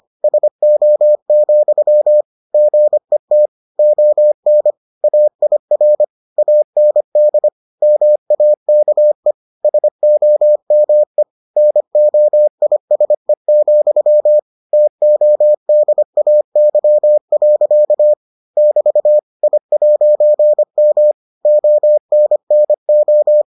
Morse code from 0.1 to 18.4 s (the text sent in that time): SO, GET ON AIR AND MAKE SOME NOISE, TODAY.